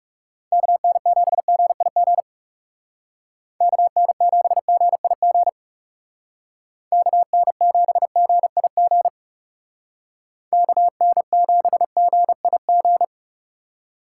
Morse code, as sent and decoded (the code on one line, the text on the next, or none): -.- -. --... --. .. --. / -.- -. --... --. .. --. / -.- -. --... --. .. --. / -.- -. --... --. .. --.
KN7GIG KN7GIG KN7GIG KN7GIG